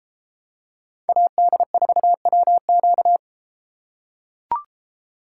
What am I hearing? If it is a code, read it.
Morse code, heard as a keyed tone, about 32 wpm, 700 Hz. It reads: AD4WQ